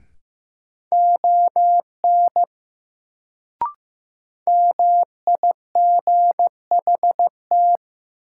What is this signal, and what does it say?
Morse code, heard as a keyed tone, about 15 wpm, 700 Hz. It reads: ON MIGHT
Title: Sets of 2 Words - New Word Drill - 15wpm